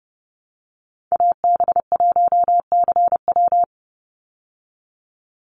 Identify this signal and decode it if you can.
Morse code, heard as a keyed tone, about 30 wpm, 700 Hz. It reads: AB1CW
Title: Call Signs Repeated 3x - 30wpm